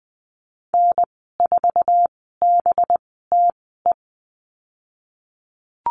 Morse code, heard as a keyed tone, about 20 wpm, spaced as for 15 wpm, 700 Hz. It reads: N4BTE
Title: 186 Lesson 30g - Letter b - Call Signs 20wpm